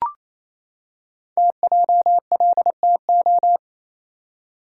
Morse code, heard as a keyed tone, about 28 wpm, 700 Hz. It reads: TJLTO
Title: Code Groups - 5 Letters - 28wpm